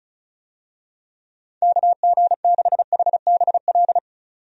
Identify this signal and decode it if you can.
Morse code, heard as a keyed tone, about 34 wpm, 700 Hz. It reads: KG6HBL
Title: USA Call Signs - 6 Characters - 35wpm